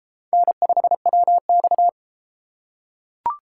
Morse code, heard as a keyed tone, about 32 wpm, 700 Hz. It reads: N5WX